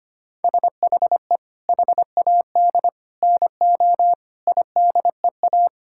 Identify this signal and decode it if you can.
Morse code, heard as a keyed tone, about 25 wpm, 700 Hz. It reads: SHE HAD NO IDEA
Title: Sentences from Top 1000 Words - Review Part 6 - 25wpm